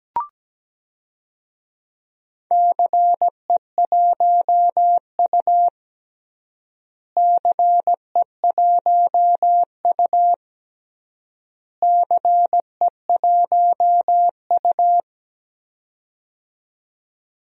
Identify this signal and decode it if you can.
Morse code, heard as a keyed tone, about 17 wpm, 700 Hz. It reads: CE1U CE1U CE1U
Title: DXCC-100 Call Signs - 4 Characters - Repeated 3x - 17wpm